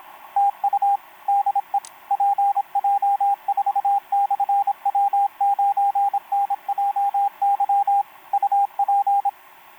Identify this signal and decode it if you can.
Morse code, heard as a keyed tone, about 27 wpm, 800 Hz, with radio noise. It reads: TU DE PJ4/W9NJY UP
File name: pj4-w9njy f5upo etc 12m cw 28mar2011 1924